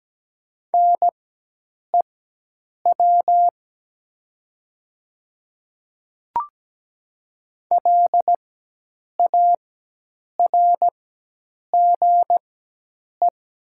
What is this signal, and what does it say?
Morse code, heard as a keyed tone, about 17 wpm, spaced as for 8 wpm, 700 Hz. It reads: NEW LARGE
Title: 4x Character Spacing - Top 200 Words - 17wpm